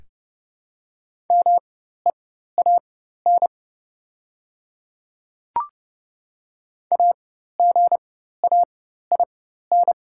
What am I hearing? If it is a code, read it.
Morse code, heard as a keyed tone, about 30 wpm, spaced as for 14 wpm, 700 Hz. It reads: MEAN AGAIN
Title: 4x Character Spacing - Top 200 Words - 30wpm